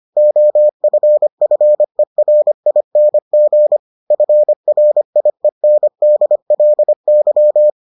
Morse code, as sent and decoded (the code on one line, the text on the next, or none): --- ..-. ..-. .-. .. -. --. / ..-. .-. .. -. -.. .-.. -.--
OFFERING FRIENDLY